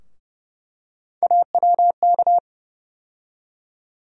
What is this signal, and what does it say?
Morse code, heard as a keyed tone, about 31 wpm, 700 Hz. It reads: AWK
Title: USA Call Sign Postfixes - 3 Letters 30wpm